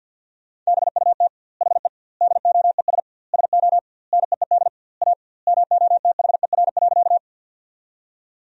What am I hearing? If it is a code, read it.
Morse code, heard as a keyed tone, about 50 wpm, 700 Hz. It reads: BUT HE DOES SO NEED A MOTHER.
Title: 2x Word-Spacing - Sentences from Top 200 Words - 50wpm